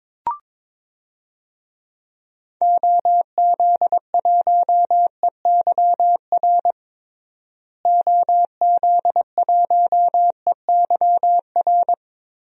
Morse code, heard as a keyed tone, about 22 wpm, 700 Hz. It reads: OZ1EYR OZ1EYR
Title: DXCC-100 Call Signs - 6 Characters - Repeated 2x - 22wpm